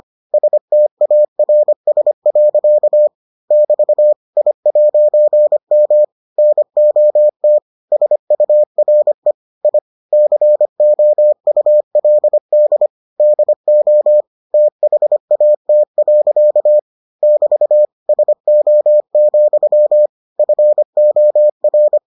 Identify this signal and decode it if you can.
Morse code, heard as a keyed tone, about 25 wpm, 600 Hz. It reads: STARS. <BT> I'M NOT SURE I COULD DO THAT. <BT> SO, FOR